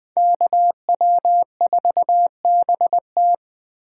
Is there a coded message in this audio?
Yes: Morse code from 0.2 to 3.3 s, heard as a keyed tone, about 21 wpm, 700 Hz. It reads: KW4BT